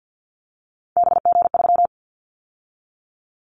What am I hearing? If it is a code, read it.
Morse code, heard as a keyed tone, about 50 wpm, 700 Hz. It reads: BG3
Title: Callsign Prefixes for Top 100 DXCC - 50wpm